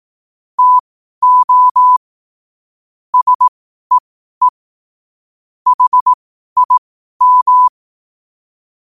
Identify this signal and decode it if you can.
Morse code, heard as a keyed tone, about 18 wpm, spaced as for 12 wpm, 1000 Hz. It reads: TO SEE HIM